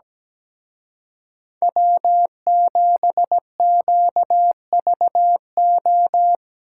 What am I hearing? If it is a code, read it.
Morse code, heard as a keyed tone, about 20 wpm, 700 Hz. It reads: W7QVO
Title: Speed Racing - USA 1x3 Call Signs 17wpm